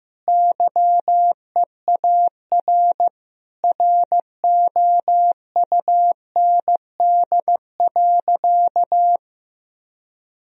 Morse code, heard as a keyed tone, about 15 wpm, 700 Hz. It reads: YEAR ROUND.
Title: Sentences from Top 1000 Words - Review Part 1 - 15wpm